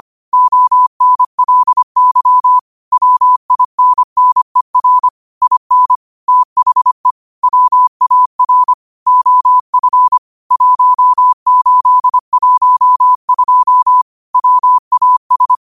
Morse code, heard as a keyed tone, about 25 wpm, 1000 Hz. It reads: ONLY WINNER IN THE WAR OF 1812 WAS